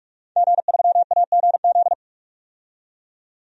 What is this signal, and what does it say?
Morse code, heard as a keyed tone, about 46 wpm, 700 Hz. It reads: G3AGZ
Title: DXCC-100 Call Signs - 5 Characters - 45wpm